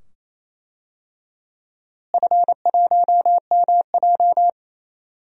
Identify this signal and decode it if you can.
Morse code, heard as a keyed tone, about 28 wpm, 700 Hz. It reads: F1MJ
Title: DXCC-100 Call Signs - 4 Characters - Repeated 2x - 28wpm